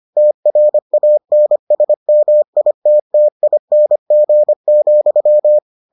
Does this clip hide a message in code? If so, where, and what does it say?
Yes: Morse code from 0.2 to 5.6 s, heard as a keyed tone, about 25 wpm, 600 Hz. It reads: TRANSMITTING,